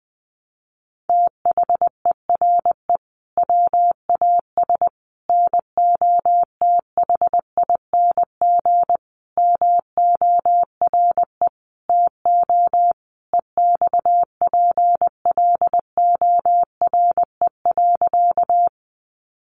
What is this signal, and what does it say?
Morse code, heard as a keyed tone, about 19 wpm, 700 Hz. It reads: THERE WAS NOTHING MORE TO EXPLORE.
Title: Sentences from Top 1100 Words 20wpm